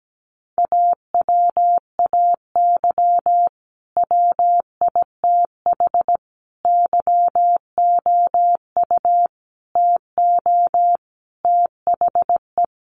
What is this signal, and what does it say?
Morse code, heard as a keyed tone, about 17 wpm, 700 Hz. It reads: AWAY WITH YOU TO THE
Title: Sentences from Top 1400 Words 17wpm